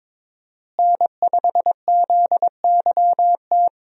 Morse code, heard as a keyed tone, about 22 wpm, 700 Hz. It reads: N5ZYT
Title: USA Call Signs - 5 Characters - Repeated 2x - 22wpm